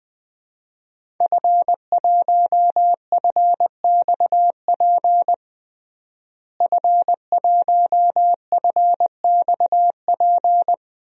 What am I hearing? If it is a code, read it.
Morse code, heard as a keyed tone, about 20 wpm, 700 Hz. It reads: F1FXP F1FXP
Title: DXCC-100 Call Signs - 5 Characters - Repeated 3x - 20wpm